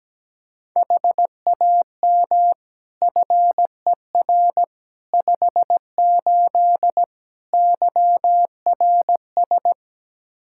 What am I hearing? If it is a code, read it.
Morse code, heard as a keyed tone, about 17 wpm, 700 Hz. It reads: HAM FER 58 YRS
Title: K1USN SST QSO Practice 17wpm